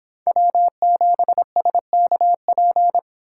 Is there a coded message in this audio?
Yes: Morse code from 0.3 to 3.0 s, heard as a keyed tone, about 25 wpm, 700 Hz. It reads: W7SKP